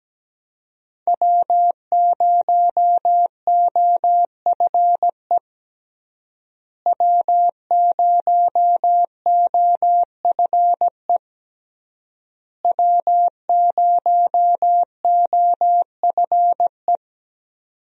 Morse code, heard as a keyed tone, about 17 wpm, 700 Hz. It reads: W0OFE W0OFE W0OFE
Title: USA Call Signs - 5 Characters - Repeated 3x - 17wpm